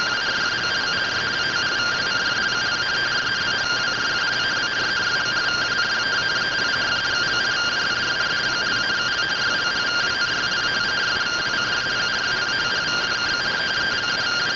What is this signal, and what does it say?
Two-tone teleprinter signal, about 170 Hz shift.
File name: findings / SAGITAL fr15 nb203 3mm.gen-33 rtty rx